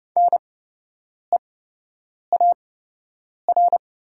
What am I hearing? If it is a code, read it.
Morse code, heard as a keyed tone, about 30 wpm, spaced as for 8 wpm, 700 Hz. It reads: NEAR
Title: 8x Character Spacing - Top 200 Words - 30wpm